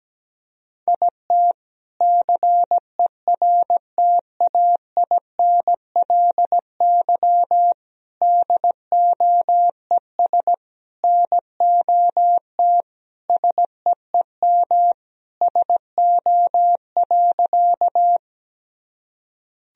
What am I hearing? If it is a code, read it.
Morse code, heard as a keyed tone, about 17 wpm, 700 Hz. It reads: IT CERTAINLY DOES NOT SEEM SO.
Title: Sentences from Top 1000 Words - Review Part 1 - 17wpm